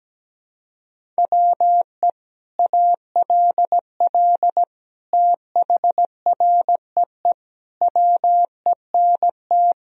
Morse code, heard as a keyed tone, about 17 wpm, 700 Hz. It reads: WE ALL THREE WENT
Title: Sentences from Top 1000 Words - Review Part 1 - 17wpm